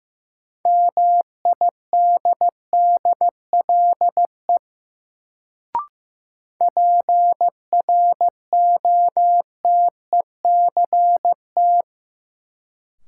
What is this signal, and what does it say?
Morse code, heard as a keyed tone, about 15 wpm, 700 Hz. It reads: MIDDLE PROTECT
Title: Sentences from Top 900 Words - New Word Drill - 15wpm